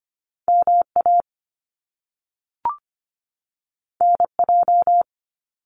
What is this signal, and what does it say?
Morse code, heard as a keyed tone, about 25 wpm, 700 Hz. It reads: MA NJ
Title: US States 25wpm